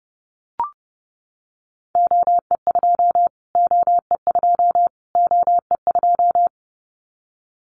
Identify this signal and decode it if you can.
Morse code, heard as a keyed tone, about 30 wpm, 690 Hz. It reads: OE2 OE2 OE2
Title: Callsign Prefixes for Top 25 DXCC - Repeated 3x - 30wpm